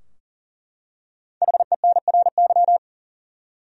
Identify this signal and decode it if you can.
Morse code, heard as a keyed tone, about 40 wpm, 700 Hz. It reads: HENRY
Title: USA Common Names - Max 5 Letters - 40wpm